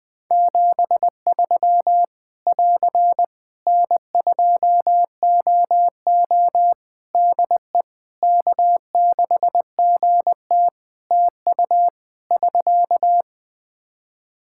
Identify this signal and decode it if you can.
Morse code, heard as a keyed tone, about 21 wpm, 700 Hz. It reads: 73 <AR> N2OO DE K6GT TU <SK>
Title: Practice QSO - Exchange 4 - Ending - 20wpm